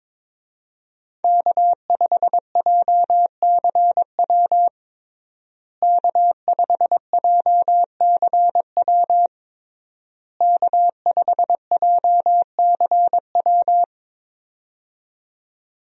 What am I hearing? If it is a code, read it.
Morse code, heard as a keyed tone, about 22 wpm, 700 Hz. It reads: K5JCW K5JCW K5JCW
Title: USA Call Signs - 5 Characters - Repeated 3x - 22wpm